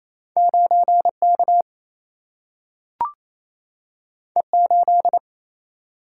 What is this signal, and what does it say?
Morse code, heard as a keyed tone, about 28 wpm, 700 Hz. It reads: 9K E8
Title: Single-Letter Single-Number 28wpm